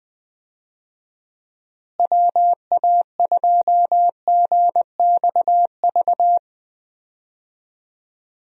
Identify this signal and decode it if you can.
Morse code, heard as a keyed tone, about 20 wpm, 700 Hz. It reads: WA2GXV
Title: USA Call Signs - 6 Characters - Repeated 2x - 20wpm